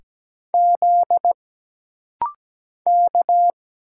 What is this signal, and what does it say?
Morse code, heard as a keyed tone, about 17 wpm, 700 Hz. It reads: Z K
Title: Single Letters - Rapid-Fire 17wpm